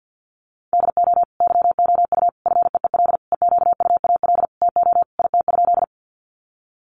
Morse code, heard as a keyed tone, about 50 wpm, 700 Hz. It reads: DO YOU FEEL EQUAL TO IT?
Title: Sentences from Top 1400 Words 50wpm